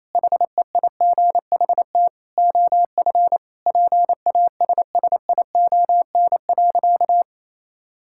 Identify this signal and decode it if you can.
Morse code, heard as a keyed tone, about 28 wpm, 700 Hz. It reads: HEIGHT OF PASSION.
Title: Sentences from Top 1600 Words 28wpm